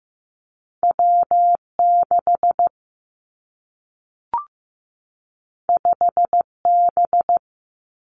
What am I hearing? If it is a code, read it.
Morse code, heard as a keyed tone, about 15 wpm, 700 Hz. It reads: W6 5B
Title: Two Challenging Letter-Number Combinations 15wpm